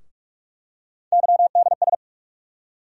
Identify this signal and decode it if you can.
Morse code, heard as a keyed tone, about 45 wpm, 700 Hz. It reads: YDS
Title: USA Call Sign Postfixes - 3 Letters 45wpm